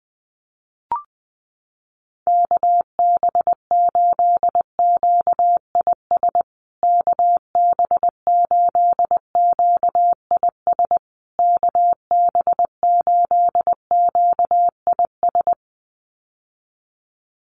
Morse code, heard as a keyed tone, about 20 wpm, 700 Hz. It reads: KB8QIS KB8QIS KB8QIS
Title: USA Call Signs Repeated 3x 20wpm